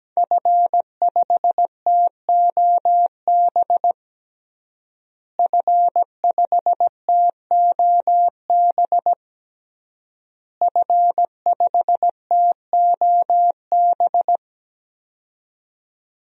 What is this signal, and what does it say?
Morse code, heard as a keyed tone, about 17 wpm, 700 Hz. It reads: F5TOB F5TOB F5TOB
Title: DXCC-100 Call Signs - 5 Characters - Repeated 3x - 17wpm